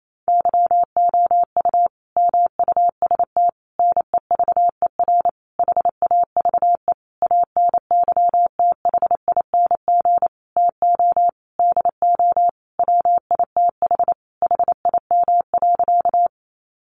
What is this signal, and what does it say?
Morse code, heard as a keyed tone, about 28 wpm, 700 Hz. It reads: YOU MUST NEVER HAVE ANYTHING TO DO WITH HIM.
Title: Sentences from Top 1000 Words 28wpm